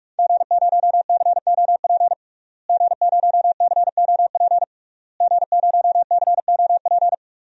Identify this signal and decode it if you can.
Morse code, heard as a keyed tone, about 44 wpm, 700 Hz. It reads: G0COP G0COP G0COP
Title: DXCC-100 Call Signs - 5 Characters - Repeated 3x - 45wpm